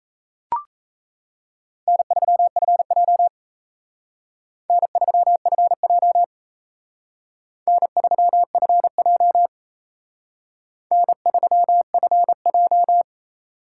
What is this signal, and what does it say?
Morse code, heard as a keyed tone, about 36 wpm, 700 Hz. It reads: N3FJ N3FJ N3FJ N3FJ